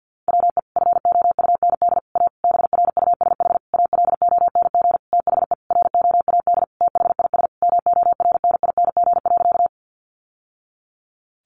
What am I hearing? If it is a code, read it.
Morse code, heard as a keyed tone, about 50 wpm, 700 Hz. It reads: WE FOUND A BRUSH ALONG THE ROAD THIS MORNING.
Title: Sentences from Top 1100 Words 50wpm